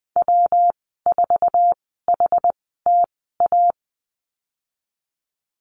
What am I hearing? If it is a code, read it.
Morse code, heard as a keyed tone, about 20 wpm, spaced as for 15 wpm, 700 Hz. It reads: W4HTA
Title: 173 Lesson 28g - Number 9 - Call Signs 20wpm